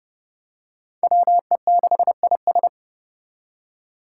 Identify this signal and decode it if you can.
Morse code, heard as a keyed tone, about 30 wpm, 700 Hz. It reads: WE6IS
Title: Five Challenging Letter-Number Combinations 30wpm